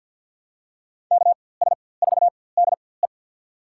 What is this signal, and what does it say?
Morse code, heard as a keyed tone, about 46 wpm, spaced as for 23 wpm, 700 Hz. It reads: KS4DE
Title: USA Call Signs - 5 Characters - 4x Character Spacing - 50wpm